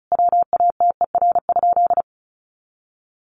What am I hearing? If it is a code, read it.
Morse code, heard as a keyed tone, about 34 wpm, 700 Hz. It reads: WATER?
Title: Sentences from Top 200 Words 35wpm